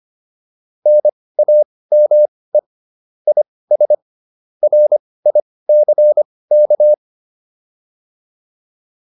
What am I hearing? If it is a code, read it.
Morse code, heard as a keyed tone, about 25 wpm, spaced as for 18 wpm, 600 Hz. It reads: NAME IS RICK
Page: intermediate-practice-files